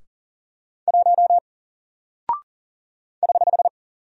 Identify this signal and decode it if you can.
Morse code, heard as a keyed tone, about 40 wpm, 700 Hz. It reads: 1 <HH>
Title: Single Character (Letter:Number:Punctuation:Prosign) Alt 40wpm